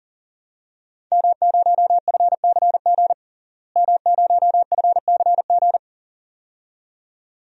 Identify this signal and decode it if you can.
Morse code, heard as a keyed tone, about 40 wpm, 700 Hz. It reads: M0FCG M0FCG
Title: DXCC-100 Call Signs - 5 Characters - Repeated 2x - 40wpm